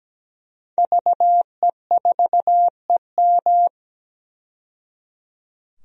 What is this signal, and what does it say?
Morse code, heard as a keyed tone, about 17 wpm, 700 Hz. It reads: VE4EM
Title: DXCC-100 Call Signs - 5 Characters - 17wpm